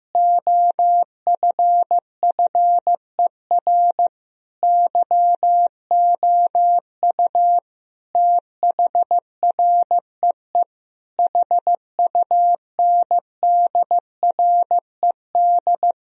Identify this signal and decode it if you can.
Morse code, heard as a keyed tone, about 15 wpm, 700 Hz. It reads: OFFER YOU THREE HUNDRED
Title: Sentences from Top 1000 Words - Review Part 4 - 15wpm